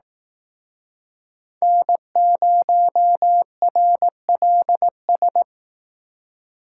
Morse code, heard as a keyed tone, about 19 wpm, 700 Hz. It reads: N0RLS